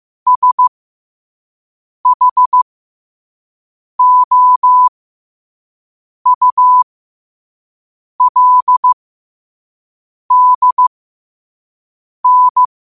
Morse code, heard as a keyed tone, about 15 wpm, spaced as for 5 wpm, 1000 Hz. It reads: SHOULDN
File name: qotdfiles / 5wpm